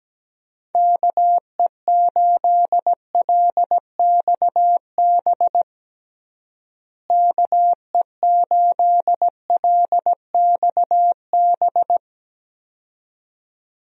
Morse code, heard as a keyed tone, about 17 wpm, 700 Hz. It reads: KE8LXB KE8LXB
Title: USA Call Signs - 6 Characters - Repeated 2x - 17wpm